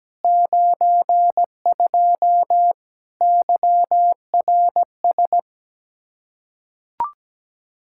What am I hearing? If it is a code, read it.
Morse code, heard as a keyed tone, about 17 wpm, 700 Hz. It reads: 92 YRS
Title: Practice QSO - Exchange 3 - More Info - 17wpm